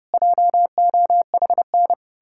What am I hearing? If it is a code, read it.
Morse code, heard as a keyed tone, about 39 wpm, 700 Hz. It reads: JOHN